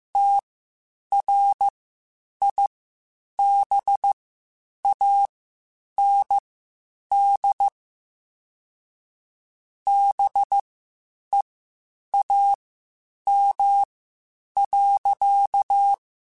Morse code, heard as a keyed tone, about 15 wpm, spaced as for 8 wpm, 800 Hz. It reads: TRIBAND BEAM.
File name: (spaced) / QSOs Spaced-mp3s / WD0BGP 1 @13 sp